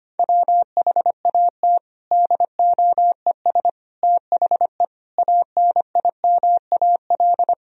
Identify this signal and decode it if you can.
Morse code, heard as a keyed tone, about 25 wpm, 700 Hz. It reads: WHAT DOES THE ANIMAL
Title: Sentences from Top 1000 Words - Review Part 1 - 25wpm